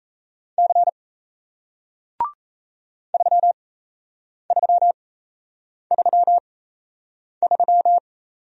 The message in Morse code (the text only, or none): C 3333